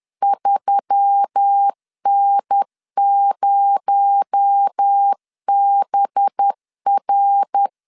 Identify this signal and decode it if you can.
Morse code, heard as a keyed tone, about 10 wpm, 800 Hz. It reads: 3N0BR
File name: LEZIONI / 7 GF9Z / 7 GF9Z 10wpm 2611